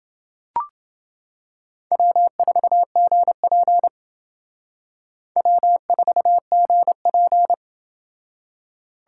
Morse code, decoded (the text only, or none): W4GP W4GP